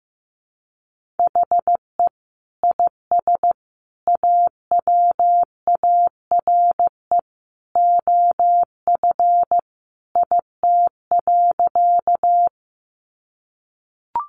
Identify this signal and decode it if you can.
Morse code, heard as a keyed tone, about 15 wpm, 700 Hz. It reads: HE IS AWARE OF IT.
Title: Sentences from Top 1200 Words 15wpm